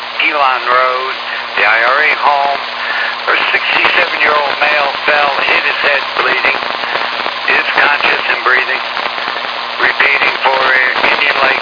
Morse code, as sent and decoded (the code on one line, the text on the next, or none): none